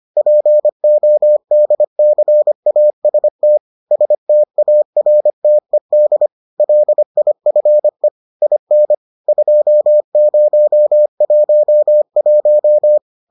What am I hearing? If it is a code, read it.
Morse code, heard as a keyed tone, about 25 wpm, 600 Hz. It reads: PODCAST STARTED LIFE IN 2011